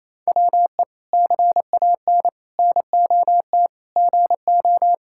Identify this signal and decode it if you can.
Morse code, heard as a keyed tone, about 27 wpm, 700 Hz. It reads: WE CAN NOT GO
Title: Sentences from Top 1000 Words - Review Part 6 - 28wpm